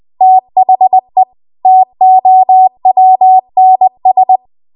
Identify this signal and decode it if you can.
Morse code, heard as a keyed tone, about 20 wpm, 750 Hz. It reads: THE TOWNS
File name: Morse / 20WPM